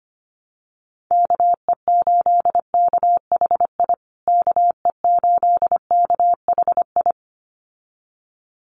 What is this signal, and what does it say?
Morse code, heard as a keyed tone, about 25 wpm, 700 Hz. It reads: KE8KHI KE8KHI
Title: USA Call Signs Repeated 2x 25wpm